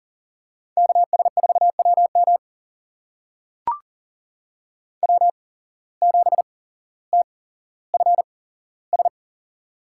Morse code, heard as a keyed tone, about 42 wpm, spaced as for 17 wpm, 700 Hz. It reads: KS4WM W7TFS
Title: USA Call Signs - 5 Characters - 8x Character Spacing - 40wpm